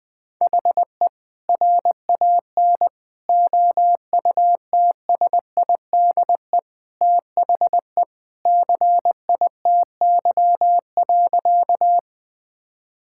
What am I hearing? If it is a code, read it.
Morse code, heard as a keyed tone, about 20 wpm, 700 Hz. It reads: HE RAN OUTSIDE THE CITY.
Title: Sentences from Top 1000 Words - Review Part 6 - 20wpm